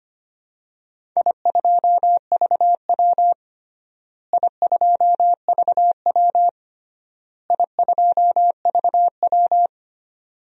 Morse code, heard as a keyed tone, about 25 wpm, 700 Hz. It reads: I2VW I2VW I2VW
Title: DXCC-100 Call Signs - 4 Characters - Repeated 3x - 25wpm